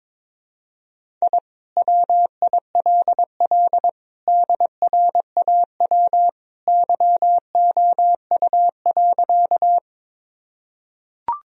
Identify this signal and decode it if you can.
Morse code, heard as a keyed tone, about 22 wpm, 700 Hz. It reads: I WILL DRAW YOU.
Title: Sentences from Top 1000 Words - Review Part 1 - 22wpm